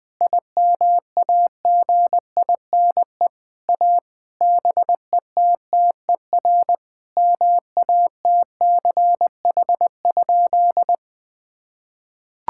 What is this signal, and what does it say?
Morse code, heard as a keyed tone, about 20 wpm, 700 Hz. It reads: IMAGINE A BETTER MATCH?
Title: Sentences from Top 1600 Words 20wpm